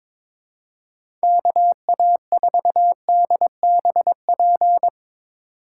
Morse code, heard as a keyed tone, about 22 wpm, 700 Hz. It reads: KA4DBP
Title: USA Call Signs - 6 Characters - 3x Character Spacing - 22wpm